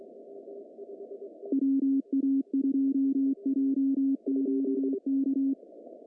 Morse code, heard as a keyed tone, about 24 wpm, 270 Hz. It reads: WA2JQK
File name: CW / cqww-cw-k5tr